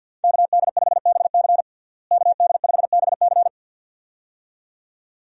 Morse code, heard as a keyed tone, about 50 wpm, 700 Hz. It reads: KD5BC KD5BC
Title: USA Call Signs - 5 Characters - Repeated 2x - 50wpm